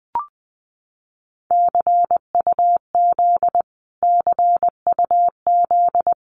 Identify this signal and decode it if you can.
Morse code, heard as a keyed tone, about 20 wpm, 700 Hz. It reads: CUZ CUZ
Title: Abbreviations Repeated 3x - 20wpm